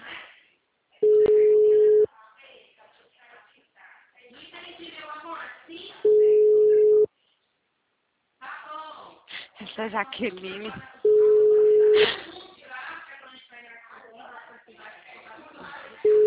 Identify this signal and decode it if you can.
Ringback tone (Europe, 425 Hz), with speech.
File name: out- -2014- - - .99307